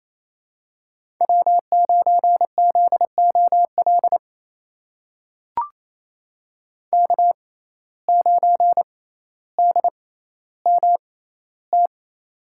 Morse code, heard as a keyed tone, about 28 wpm, spaced as for 14 wpm, 700 Hz. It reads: W9ZOL K9DMT
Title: USA Call Signs - 5 Characters - 6x Character Spacing - 28wpm